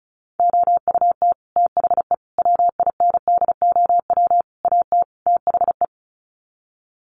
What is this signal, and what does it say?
Morse code, heard as a keyed tone, about 34 wpm, 700 Hz. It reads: OUT THE WINDOW AT THE
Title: Sets of 5 Words Encore2 35wpm